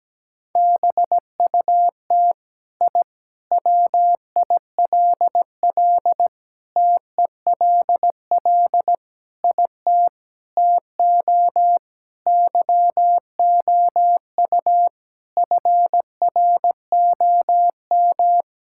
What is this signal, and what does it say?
Morse code, heard as a keyed tone, about 17 wpm, 700 Hz. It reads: BUT I WILL TELL IT TO YOU FROM
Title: Sentences from Top 1000 Words - Review Part 5 - 17wpm